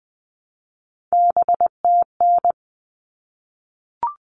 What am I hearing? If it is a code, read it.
Morse code, heard as a keyed tone, about 20 wpm, 700 Hz. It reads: BTN
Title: All Cut Numbers - 3-Digit 20wpm